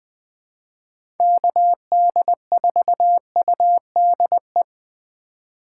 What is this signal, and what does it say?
Morse code, heard as a keyed tone, about 20 wpm, 700 Hz. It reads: KD4UDE
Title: USA Call Signs - 6 Characters - 8x Character Spacing - 20wpm